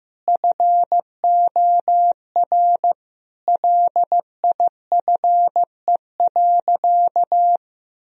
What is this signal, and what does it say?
Morse code, heard as a keyed tone, about 15 wpm, 700 Hz. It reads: FOR LIFE.
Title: Sentences from Top 1000 Words - Review Part 4 - 15wpm